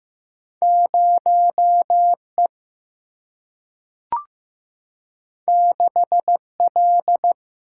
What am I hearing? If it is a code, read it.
Morse code, heard as a keyed tone, about 15 wpm, 700 Hz. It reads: 0E 6L
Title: Single-Letter Single-Number 15wpm